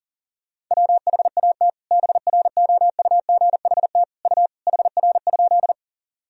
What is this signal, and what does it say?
Morse code, heard as a keyed tone, about 40 wpm, 700 Hz. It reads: WHAT BROUGHT U HR?
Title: Sentences from Top 500 Words and Abbreviations 40wpm